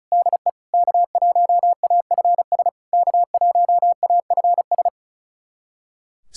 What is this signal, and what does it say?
Morse code, heard as a keyed tone, about 36 wpm, 690 Hz. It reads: DE K1AFS K1AFS
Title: SOTA QSO Practice 35wpm